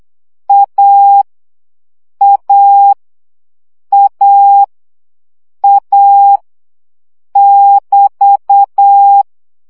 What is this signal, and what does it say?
Morse code, heard as a keyed tone, about 8 wpm, 800 Hz. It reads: A A A A <BT>
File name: CW gyakorlas a 7 baud